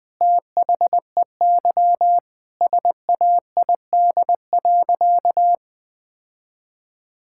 Morse code, heard as a keyed tone, about 20 wpm, 700 Hz. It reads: THEY SAID.
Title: Sentences with 2-Word Phrases - 20wpm